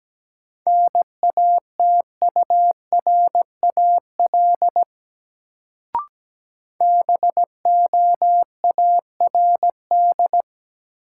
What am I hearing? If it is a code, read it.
Morse code, heard as a keyed tone, about 17 wpm, 700 Hz. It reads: NATURAL BOARD
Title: Sentences from Top 700 Words - New Word Drill - 17wpm